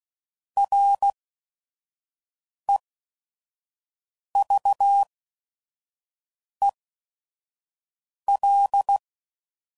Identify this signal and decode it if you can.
Morse code, heard as a keyed tone, about 16 wpm, spaced as for 5 wpm, 800 Hz. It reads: REVEL
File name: Prefixes / By First Letter / Rs